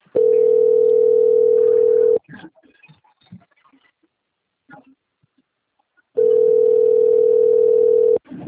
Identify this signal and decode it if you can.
Ringback tone (North America), 440 + 480 Hz.
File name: out- - - - - .72991